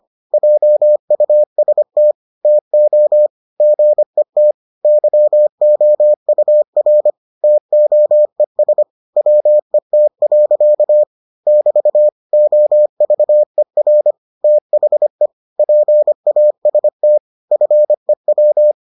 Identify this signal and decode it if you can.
Morse code, heard as a keyed tone, about 25 wpm, 600 Hz. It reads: JUST TO GET YOUR TOES WET. <BT> OVER THE PAST FEW